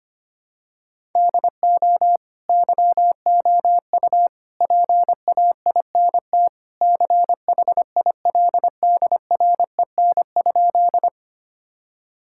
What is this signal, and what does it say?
Morse code, heard as a keyed tone, about 25 wpm, 700 Hz. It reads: DO YOU PAINT CHILDREN?
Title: Sentences from Top 1000 Words - Review Part 6 - 25wpm